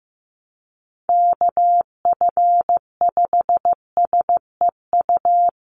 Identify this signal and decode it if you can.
Morse code, heard as a keyed tone, about 15 wpm, 700 Hz. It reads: KF5SEU
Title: USA Call Signs 15wpm